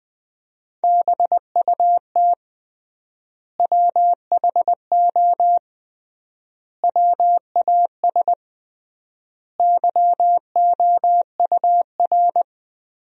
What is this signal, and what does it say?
Morse code, heard as a keyed tone, about 20 wpm, 700 Hz. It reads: BUT WHO WAS YOUR
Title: 3x Word-Spacing - Sentences from Top 200 Words - 20wpm